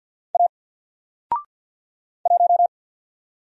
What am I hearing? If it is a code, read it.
Morse code, heard as a keyed tone, about 50 wpm, 700 Hz. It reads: A 1
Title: Single Letter-Number 50wpm